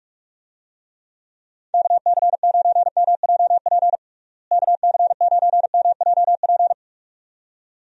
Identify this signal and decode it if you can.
Morse code, heard as a keyed tone, about 46 wpm, 700 Hz. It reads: KC9MJP KC9MJP
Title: USA Call Signs - 6 Characters - Repeated 2x - 45wpm